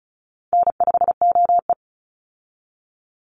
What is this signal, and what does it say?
Morse code, heard as a keyed tone, about 34 wpm, 700 Hz. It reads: N5OE
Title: Easy Call Signs Repeated 2x - 35wpm